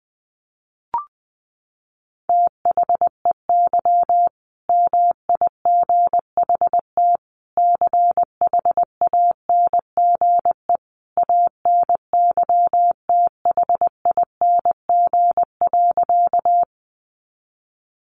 Morse code, heard as a keyed tone, about 19 wpm, 700 Hz. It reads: THEY MIGHT CHANGE ANYTHING.
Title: A Sound Of Thunder 20wpm